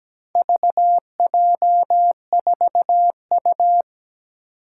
Morse code, heard as a keyed tone, about 17 wpm, 700 Hz. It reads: VJ4U
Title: Four Challenging Letter-Number Combinations 17wpm